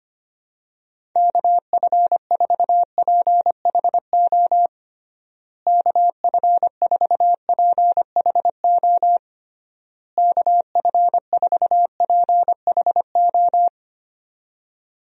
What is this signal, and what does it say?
Morse code, heard as a keyed tone, about 25 wpm, 700 Hz. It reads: KF4PHO KF4PHO KF4PHO
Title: USA Call Signs - 6 Characters - Repeated 3x - 25wpm